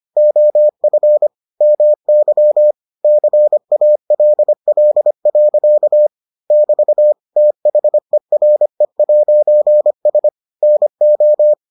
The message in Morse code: OF MY CALL. <BT> THERE'S NO

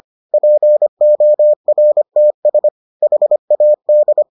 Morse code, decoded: PORTS HAD

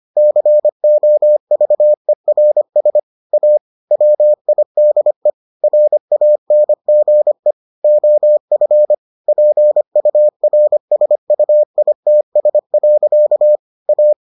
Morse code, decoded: COVERS A WIDE RANGE OF PURSUITS. A

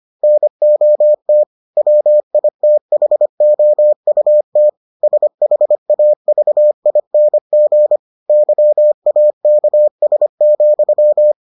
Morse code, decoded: NOT WITHOUT SHAVING YAKS,

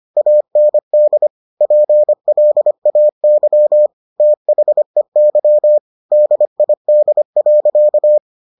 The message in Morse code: AND PLAY THEY DID.

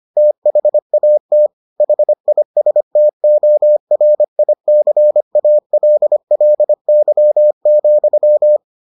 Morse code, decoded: THAT HISTORICALLY,

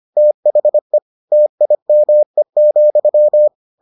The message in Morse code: THE TIME,

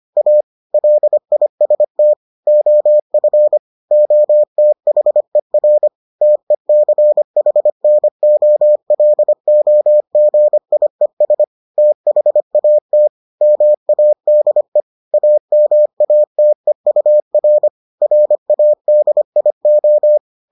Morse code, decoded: A LIST OF OTHER TECHNOLOGIES THAT MADE AMATEUR RADIO

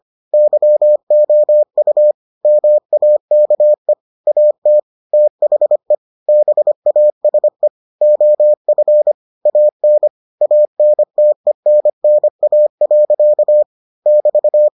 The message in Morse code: YOU MAKE AT THE BASE OF AN ANTENNA. <BT>

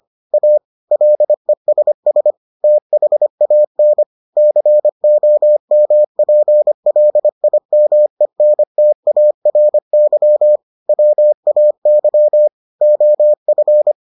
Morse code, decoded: A LESS THAN COMPLIMENTARY WAY OF

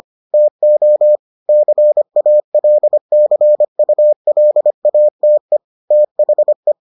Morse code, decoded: TO CALCULATE THE

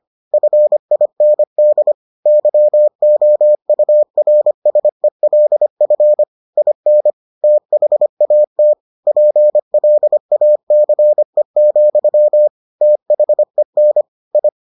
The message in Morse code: FIND YOURSELF IN THAT PLACE, THEN I

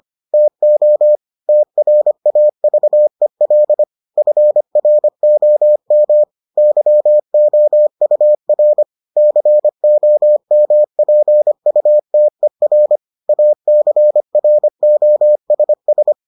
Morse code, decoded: TO TRAVEL FROM YOUR COMPUTER ACROSS

